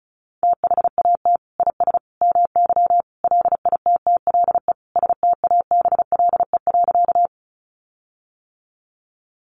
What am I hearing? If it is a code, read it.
Morse code, heard as a keyed tone, about 34 wpm, 700 Hz. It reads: THAT IS MY LITTLE STABLE.
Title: Sentences from Top 1500 Words 35wpm